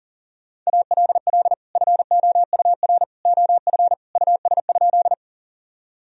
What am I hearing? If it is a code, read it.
Morse code, heard as a keyed tone, about 40 wpm, 700 Hz. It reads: ALL FOUR OF US?
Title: Sentences from Top 1000 Words - Review Part 1 - 40wpm